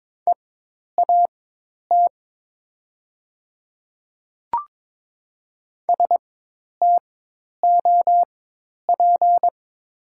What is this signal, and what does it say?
Morse code, heard as a keyed tone, about 22 wpm, spaced as for 10 wpm, 700 Hz. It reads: EAT STOP
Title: 4x Character Spacing - Top 500 Words - 22wpm